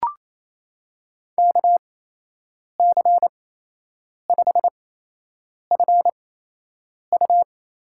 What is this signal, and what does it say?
Morse code, heard as a keyed tone, about 28 wpm, spaced as for 8 wpm, 700 Hz. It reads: KC5FU
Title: USA Call Signs - 8x Character Spacing - 28wpm